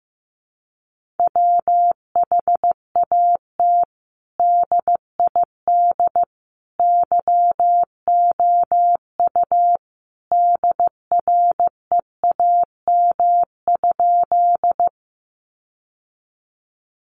Morse code, heard as a keyed tone, about 15 wpm, 700 Hz. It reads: WHAT DID YOU DREAM?
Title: Sentences from Top 1500 Words 15wpm